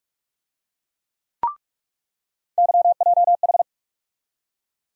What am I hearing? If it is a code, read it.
Morse code, heard as a keyed tone, about 46 wpm, 700 Hz. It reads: YJH
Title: USA Call Sign Postfixes - 3 Letters 45wpm